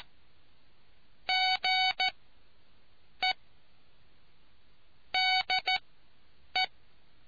Morse code, heard as a keyed tone, about 14 wpm, spaced as for 7 wpm, 750 Hz. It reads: GE DE